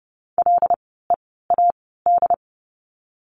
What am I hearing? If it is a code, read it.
Morse code, heard as a keyed tone, about 30 wpm, spaced as for 17 wpm, 700 Hz. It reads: LEAD
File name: Lesson 16f - Letter c - Words 30wpm